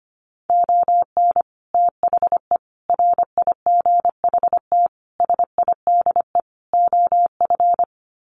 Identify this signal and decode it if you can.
Morse code, heard as a keyed tone, about 25 wpm, 700 Hz. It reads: ON THE RIGHT SIDE OF
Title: Sets of 5 Words Encore 25wpm